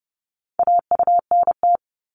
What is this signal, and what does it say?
Morse code, heard as a keyed tone, about 30 wpm, 700 Hz. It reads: AUNT
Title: Multi-letter Phonemes Repeated 3x 30wpm